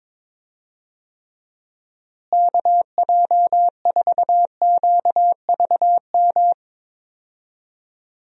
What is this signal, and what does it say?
Morse code, heard as a keyed tone, about 22 wpm, 700 Hz. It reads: KJ4QVM
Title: USA Call Signs - 6 Characters - Repeated 2x - 22wpm